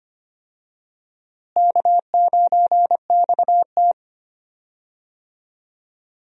Morse code, heard as a keyed tone, about 25 wpm, 700 Hz. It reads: K9XT
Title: USA Call Signs - 4 Characters - Repeated 2x - 25wpm